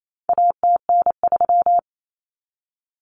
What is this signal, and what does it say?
Morse code, heard as a keyed tone, about 28 wpm, 700 Hz. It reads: ATN3
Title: Common Cut Numbers - 4-Digit 28wpm